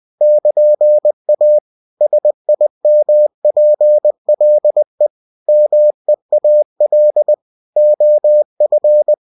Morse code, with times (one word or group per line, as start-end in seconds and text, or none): 0.2-1.6 <KN>A
2.0-5.1 SIMPLE
5.5-7.3 MEAL
7.8-9.1 OF